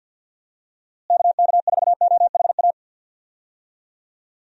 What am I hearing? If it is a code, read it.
Morse code, heard as a keyed tone, about 50 wpm, 700 Hz. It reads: KK4OHA
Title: USA Call Signs - 6 Characters - Repeated 3x - 50wpm